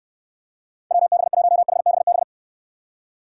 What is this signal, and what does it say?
Morse code, heard as a keyed tone, about 64 wpm, 700 Hz. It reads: WB1HBB